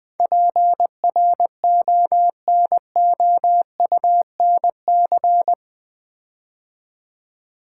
Morse code, heard as a keyed tone, about 21 wpm, 700 Hz. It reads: PRONOUNC
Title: Build A Word - 4 Letters At A Time - 10-Letter Words - 20wpm